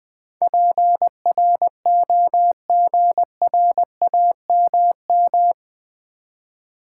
Morse code, heard as a keyed tone, about 20 wpm, 700 Hz. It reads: PROGRAMM
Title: Build A Word - 4 Letters At A Time - 11-Letter Words - 20wpm